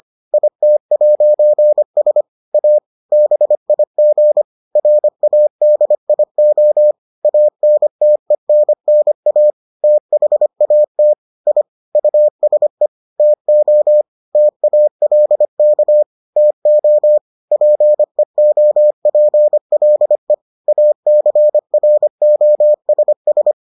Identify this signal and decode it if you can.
Morse code, heard as a keyed tone, about 25 wpm, 600 Hz. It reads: IT'S A BIG RADIO ANTENNA THAT I USE TO TALK TO PEOPLE ACROSS